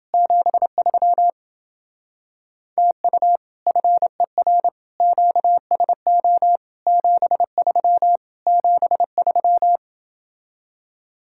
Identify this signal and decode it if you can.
Morse code, heard as a keyed tone, about 30 wpm, 700 Hz. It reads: 73 TU FER QSO 73 73